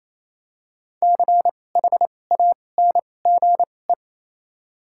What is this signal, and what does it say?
Morse code, heard as a keyed tone, about 28 wpm, 700 Hz. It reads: C H A N G E
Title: 2x Character Spacing - Top 200 Words - 28wpm